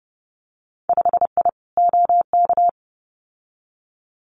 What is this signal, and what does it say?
Morse code, heard as a keyed tone, about 30 wpm, 700 Hz. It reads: 5I OK
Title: Winter Field Day Exchanges Encore 30wpm